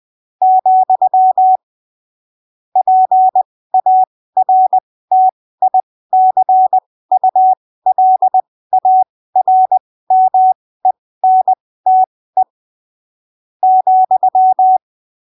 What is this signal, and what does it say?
Morse code, heard as a keyed tone, about 20 wpm, spaced as for 15 wpm, 750 Hz. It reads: , PARTICULARMENTE ,
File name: arquivos / texto / texto 25 20ppm